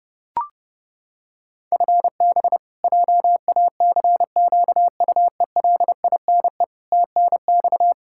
Morse code, heard as a keyed tone, about 30 wpm, 700 Hz. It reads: FB JACQUELINE TNX
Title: Practice QSO - Exchange 3 - More Info - 30wpm